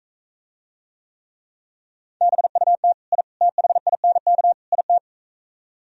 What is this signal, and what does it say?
Morse code, heard as a keyed tone, about 45 wpm, 700 Hz. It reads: BUT I THINK IT